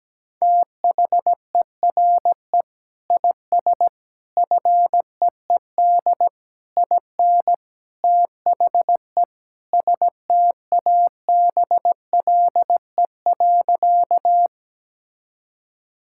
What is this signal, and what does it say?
Morse code, heard as a keyed tone, about 17 wpm, 700 Hz. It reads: THERE IS FEED IN THE STABLE.
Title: Sentences from Top 1600 Words 17wpm